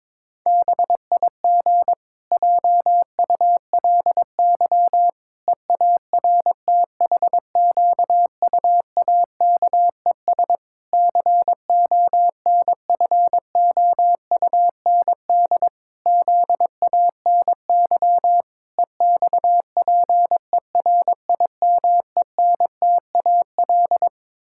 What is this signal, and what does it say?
Morse code, heard as a keyed tone, about 22 wpm, 700 Hz. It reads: BIG JULY EARTHQUAKES CONFOUND ZANY EXPERIMENTAL
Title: Pangram Challenge v2 - 22wpm